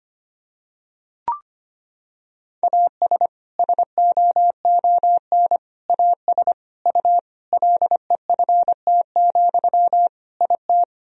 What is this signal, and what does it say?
Morse code, heard as a keyed tone, about 25 wpm, 700 Hz. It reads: AS SOON AS U LEFT, IT
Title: Sentences from Top 400 Words and Abbreviations 25wpm